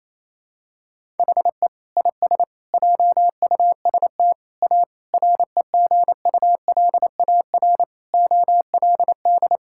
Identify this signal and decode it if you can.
Morse code, heard as a keyed tone, about 28 wpm, 700 Hz. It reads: HE IS JUST A REGULAR OLD
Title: Sentences from Top 1000 Words - Review Part 6 - 28wpm